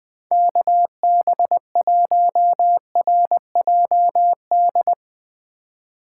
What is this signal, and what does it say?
Morse code, heard as a keyed tone, about 20 wpm, 700 Hz. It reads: KB1RJD
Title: 2023 SOTA Chaser Call Signs - 20wpm